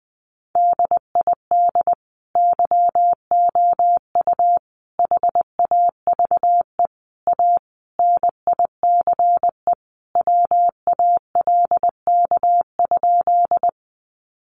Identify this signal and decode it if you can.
Morse code, heard as a keyed tone, about 20 wpm, 700 Hz. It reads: DID YOU HAVE A NICE WALK?
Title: Sentences from Top 1100 Words 20wpm